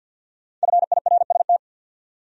Morse code, heard as a keyed tone, about 63 wpm, 700 Hz. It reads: FIRST